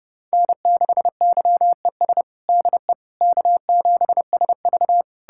Morse code, heard as a keyed tone, about 32 wpm, 700 Hz. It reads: N6YES DE K7SV
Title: K1USN SST QSO Practice 30wpm